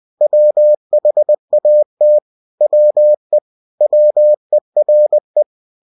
Morse code, heard as a keyed tone, about 20 wpm, 600 Hz. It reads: WHAT WE WERE